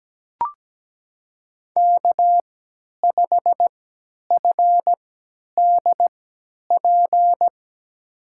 Morse code, heard as a keyed tone, about 17 wpm, spaced as for 10 wpm, 700 Hz. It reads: K5FDP
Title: USA Call Signs - 5 Characters - 3x Character Spacing - 17wpm